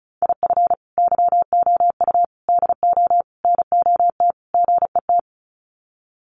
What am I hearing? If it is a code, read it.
Morse code, heard as a keyed tone, about 34 wpm, 700 Hz. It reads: IF YOU DO NOT GET
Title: Sets of 5 Words Encore2 35wpm